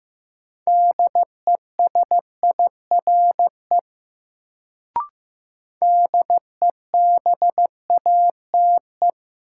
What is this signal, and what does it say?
Morse code, heard as a keyed tone, about 15 wpm, 700 Hz. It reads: DESIRE DEBATE
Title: Sentences from Top 1100 Words - New Word Drill - 15wpm